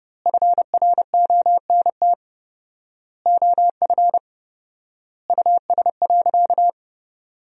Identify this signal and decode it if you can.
Morse code, heard as a keyed tone, about 29 wpm, 700 Hz. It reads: FRONT OF US.
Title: 4x Word-Spacing - Sentences from Top 500 Words - 30wpm